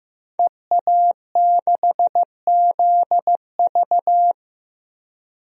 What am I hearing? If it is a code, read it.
Morse code, heard as a keyed tone, about 15 wpm, 700 Hz. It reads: EA6ZV
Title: DXCC-100 Call Signs - 5 Characters - 15wpm